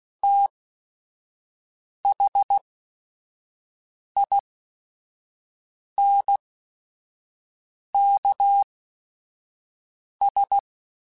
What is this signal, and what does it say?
Morse code, heard as a keyed tone, about 16 wpm, spaced as for 5 wpm, 800 Hz. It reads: THINKS